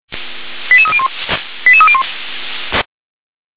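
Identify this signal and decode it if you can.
Five-tone selective call (ZVEI), heard as quick runs of five tones, tones of about 70 ms. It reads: 88301 88301
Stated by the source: ZVEI five-tone call 88301 received by SDR.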